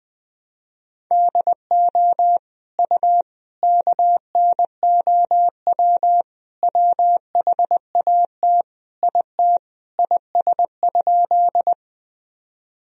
Morse code, heard as a keyed tone, about 20 wpm, 700 Hz. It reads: DO U KNOW WHAT IT IS?
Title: Sentences from Top 100 Words and Abbreviations 20wpm